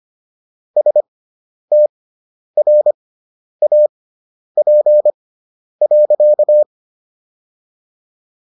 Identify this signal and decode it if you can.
Morse code, heard as a keyed tone, about 25 wpm, spaced as for 10 wpm, 600 Hz. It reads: STRAP.